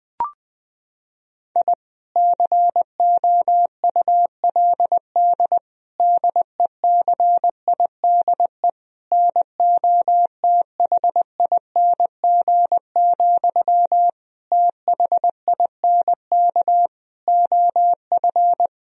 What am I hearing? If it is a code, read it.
Morse code, heard as a keyed tone, about 20 wpm, 700 Hz. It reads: I COULD DECIDE NOTHING, THINK OF
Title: Sentences from Top 1000 Words - Review Part 3 - 20wpm